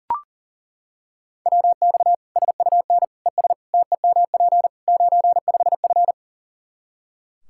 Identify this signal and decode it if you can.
Morse code, heard as a keyed tone, about 37 wpm, 700 Hz. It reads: WX SUN ES TEMP 95F
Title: K1USN SST QSO Practice 40wpm